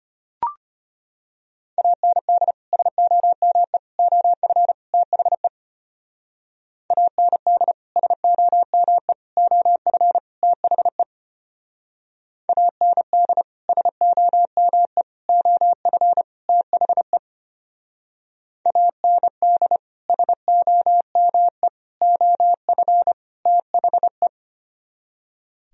Morse code, heard as a keyed tone, about 33 wpm, 700 Hz. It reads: AND SOME OF THE AND SOME OF THE AND SOME OF THE AND SOME OF THE